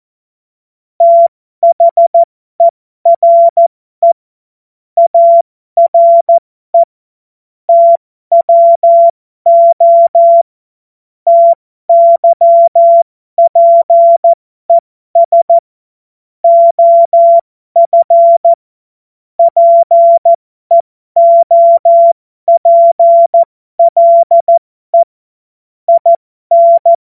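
Morse code, heard as a keyed tone, about 14 wpm, spaced as for 12 wpm, 680 Hz. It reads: THERE ARE TWO TYPES OF PEOPLE IN
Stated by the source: Morse code joke practice.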